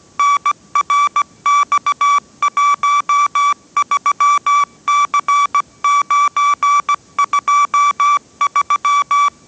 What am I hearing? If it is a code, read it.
Morse code, heard as a keyed tone, about 17 wpm, 1150 Hz. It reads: NRX13C923
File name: 18wpm